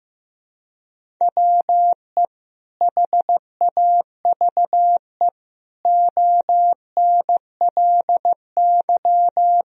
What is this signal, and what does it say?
Morse code, heard as a keyed tone, about 15 wpm, 700 Hz. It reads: WE HAVE ONLY
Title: Sentences from Top 1000 Words - Review Part 1 - 15wpm